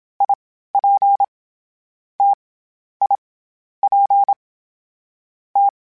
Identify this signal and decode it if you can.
Morse code, heard as a keyed tone, about 27 wpm, 800 Hz. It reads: I P T I P T
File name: Variable Speed / Variable Speed-mp3s / T AOEI's sp @25